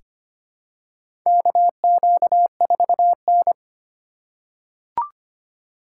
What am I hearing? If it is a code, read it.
Morse code, heard as a keyed tone, about 25 wpm, 700 Hz. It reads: KQ4N
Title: USA Call Signs - 4 Characters - 2x Character Spacing - 25wpm